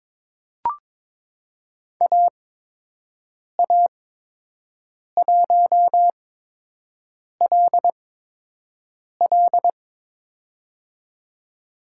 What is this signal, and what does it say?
Morse code, heard as a keyed tone, about 22 wpm, spaced as for 6 wpm, 700 Hz. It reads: AA1LL
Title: USA Call Signs - 5 Characters - 8x Character Spacing - 22wpm